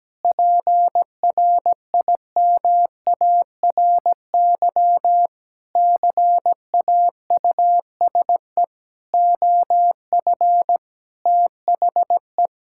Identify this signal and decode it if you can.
Morse code, heard as a keyed tone, about 17 wpm, 700 Hz. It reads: PRIMARY CAUSE OF THE
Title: Sentences from Top 1000 Words - Review Part 5 - 17wpm